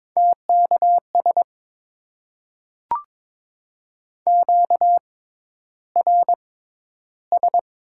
Morse code, heard as a keyed tone, about 21 wpm, spaced as for 11 wpm, 700 Hz. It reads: TKS QRS
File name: Top 90 QSO Elements - 6x Character Spacing - 22wpm